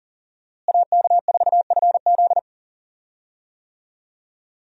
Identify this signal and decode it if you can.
Morse code, heard as a keyed tone, about 38 wpm, 700 Hz. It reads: AK4FZ
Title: USA Call Signs - 5 Characters - Repeated 2x - 40wpm